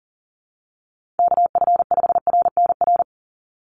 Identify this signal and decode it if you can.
Morse code, heard as a keyed tone, about 40 wpm, 700 Hz. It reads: KF5RNR
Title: USA Call Signs 40wpm